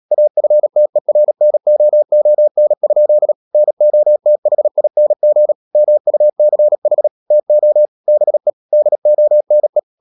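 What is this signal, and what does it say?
Morse code, heard as a keyed tone, about 37 wpm, 600 Hz. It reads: AFTERNOON? NOTHING MUCH TO BE DONE
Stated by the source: Morse practice at 36 wpm, text of Ken Follett's novel Never, from CW med Gnister.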